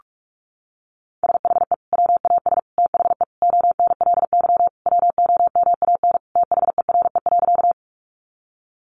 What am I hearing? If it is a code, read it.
Morse code, heard as a keyed tone, about 44 wpm, 700 Hz. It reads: SHE WAS THE ONLY WOMAN THERE.
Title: Sentences from Top 1100 Words 45wpm